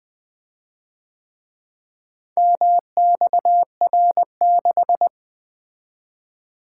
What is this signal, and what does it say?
Morse code, heard as a keyed tone, about 24 wpm, 700 Hz. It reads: MXR6
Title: Speed-Racing - Four Letter-Number Combinations - 20wpm